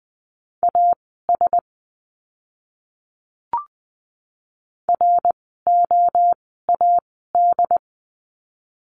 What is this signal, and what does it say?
Morse code, heard as a keyed tone, about 20 wpm, spaced as for 14 wpm, 700 Hz. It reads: AS ROAD
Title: 101 Lesson 17f - Letter u - Words 20wpm